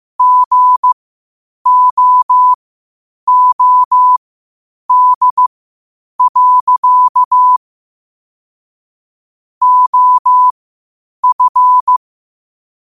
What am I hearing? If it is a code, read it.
Morse code, heard as a keyed tone, about 15 wpm, spaced as for 8 wpm, 1000 Hz. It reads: GOOD. OF
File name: qotdfiles / 8wpm